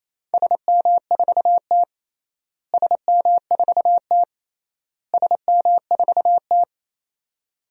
Morse code, heard as a keyed tone, about 28 wpm, 700 Hz. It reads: SM4T SM4T SM4T
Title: DXCC-100 Call Signs - 4 Characters - Repeated 3x - 28wpm